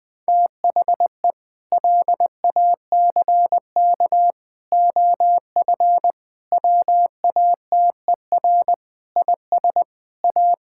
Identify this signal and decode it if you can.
Morse code, heard as a keyed tone, about 21 wpm, 700 Hz. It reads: THE LACK OF WATER IS A
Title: Sentences with 3-Word Phrases - Spoken Beginning - 20wpm